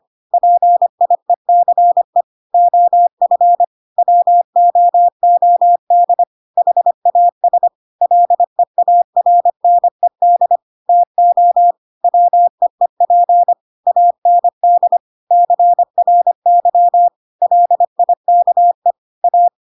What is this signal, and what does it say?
Morse code, heard as a keyed tone, about 25 wpm, 700 Hz. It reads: PIECE OF WOOD HAS LEARNED TO WEEP AND CRY LIKE A
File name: Pinocchio / Pinocchio ch1 25 25 ik1hge